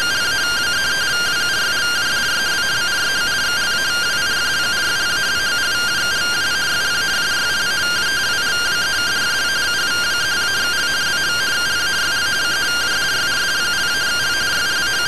Two-tone teleprinter signal, about 170 Hz shift.